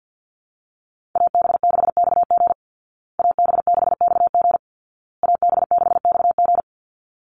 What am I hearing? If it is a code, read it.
Morse code, heard as a keyed tone, about 50 wpm, 700 Hz. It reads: AB6XG AB6XG AB6XG
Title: Call Signs Repeated 2x - 50wpm